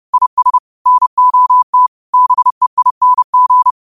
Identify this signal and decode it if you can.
Morse code, heard as a keyed tone, about 30 wpm, 1000 Hz. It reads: IS NOT BEING
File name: qotdfiles / 30wpm